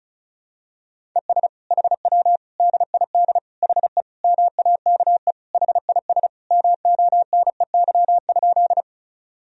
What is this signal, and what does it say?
Morse code, heard as a keyed tone, about 36 wpm, 700 Hz. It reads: ES HW DID HE MAKE HIS MONEY?
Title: Sentences from Top 400 Words and Abbreviations 35wpm